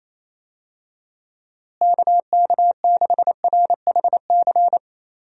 Morse code, heard as a keyed tone, about 28 wpm, 700 Hz. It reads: KK6RHC
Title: USA Call Signs - 6 Characters - Repeated 3x - 28wpm